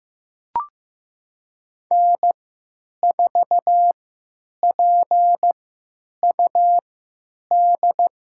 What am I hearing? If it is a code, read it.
Morse code, heard as a keyed tone, about 15 wpm, spaced as for 9 wpm, 700 Hz. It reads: N4PUD
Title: USA Call Signs - 5 Characters - 3x Character Spacing - 15wpm